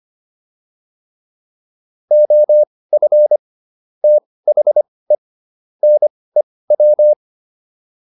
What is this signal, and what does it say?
Morse code, heard as a keyed tone, about 25 wpm, spaced as for 18 wpm, 600 Hz. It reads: OF THE NEW
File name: WD204 18